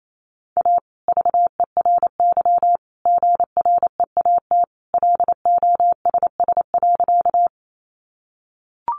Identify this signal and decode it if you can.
Morse code, heard as a keyed tone, about 28 wpm, 700 Hz. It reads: A VERY GREAT LOSS.
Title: Sentences from Top 1400 Words 28wpm